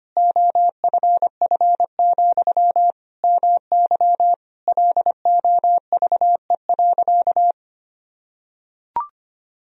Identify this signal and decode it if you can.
Morse code, heard as a keyed tone, about 25 wpm, 700 Hz. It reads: OFF, MY LOVE.
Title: Sentences from Top 1000 Words - Review Part 2 - 25wpm